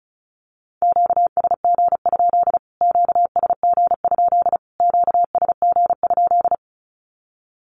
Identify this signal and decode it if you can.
Morse code, heard as a keyed tone, about 34 wpm, 700 Hz. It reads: QSG? QSG? QSG?
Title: Q-Codes Repeated 3x 35wpm